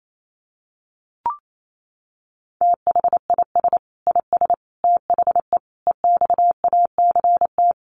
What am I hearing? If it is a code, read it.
Morse code, heard as a keyed tone, about 28 wpm, 700 Hz. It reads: THIS IS THE EXACT
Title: Sentences from Top 1300 Words 28wpm